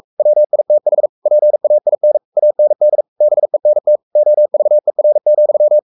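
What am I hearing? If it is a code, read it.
Morse code, heard as a keyed tone, about 44 wpm, 600 Hz. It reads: WITH PAIN AND BENT OVER,
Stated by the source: Morse code at 43 wpm sending Ken Follett's novel Never in English, from CW med Gnister.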